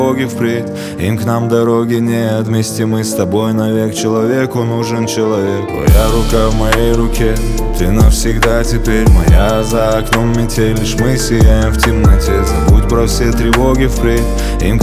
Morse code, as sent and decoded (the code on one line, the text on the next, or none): none